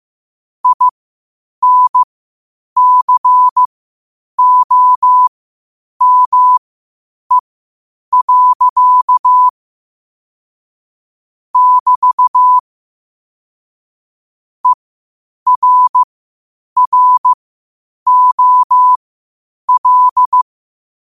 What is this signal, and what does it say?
Morse code, heard as a keyed tone, about 15 wpm, spaced as for 8 wpm, 1000 Hz. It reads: INCOME. <BT> ERROL
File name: qotdfiles / 8wpm